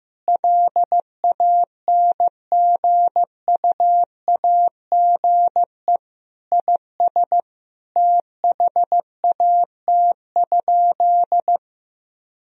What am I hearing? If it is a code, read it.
Morse code, heard as a keyed tone, about 15 wpm, 700 Hz. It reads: LANGUAGE IS THAT?
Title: Sentences from Top 1000 Words - Review Part 3 - 15wpm